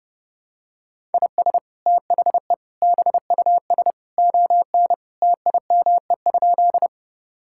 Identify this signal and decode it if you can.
Morse code, heard as a keyed tone, about 28 wpm, 700 Hz. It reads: IS THE BUS ON TIME?
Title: Sentences from Top 1000 Words - Review Part 6 - 30wpm